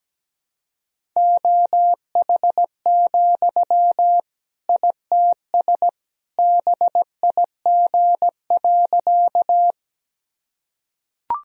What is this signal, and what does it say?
Morse code, heard as a keyed tone, about 17 wpm, 700 Hz. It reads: OH, ITS BIG.
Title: Sentences from Top 1000 Words - Review Part 1 - 17wpm